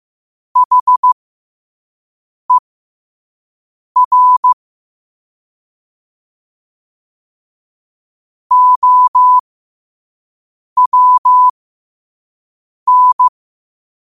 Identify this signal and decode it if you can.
Morse code, heard as a keyed tone, about 15 wpm, spaced as for 5 wpm, 1000 Hz. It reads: HER OWN
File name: qotdfiles / 5wpm